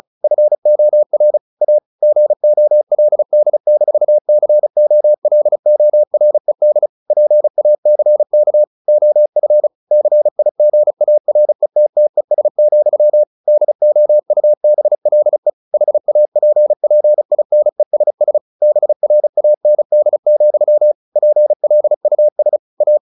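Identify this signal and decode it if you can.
Morse code, heard as a keyed tone, about 34 wpm, 600 Hz. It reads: FOR A GOLD-COLORED PACK OF CIGARETTES, DOUBLE HAPPINESS BRAND, PLUS A
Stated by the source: Morse practice at 35 wpm from CW med Gnister, sending Ken Follett's novel Never in English.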